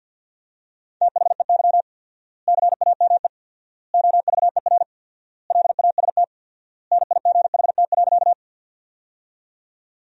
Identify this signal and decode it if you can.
Morse code, heard as a keyed tone, about 50 wpm, 700 Hz. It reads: THEY CAME OVER LAST NIGHT.
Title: 4x Word-Spacing - Sentences from Top 300 Words - 50wpm